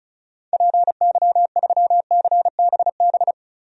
Morse code, heard as a keyed tone, about 36 wpm, 700 Hz. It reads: PY3CBB
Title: DXCC-100 Call Signs - 6 Characters - 35wpm